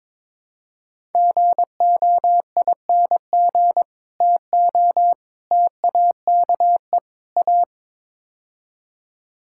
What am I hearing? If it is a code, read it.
Morse code, heard as a keyed tone, about 22 wpm, 700 Hz. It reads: GOING TO TAKE A
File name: practice / 22wpm / Build A Phrase - 5-Word Phrases - 22wpm